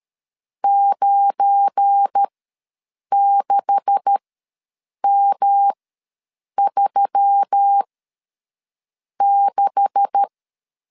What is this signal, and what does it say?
Morse code, heard as a keyed tone, about 13 wpm, spaced as for 6 wpm, 800 Hz. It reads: 96M36
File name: LEZIONI / 9 286 ws 12wpm 3351